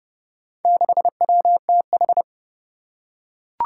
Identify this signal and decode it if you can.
Morse code, heard as a keyed tone, about 30 wpm, 700 Hz. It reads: 6WTH
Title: Four Challenging Letter-Number Combinations 30wpm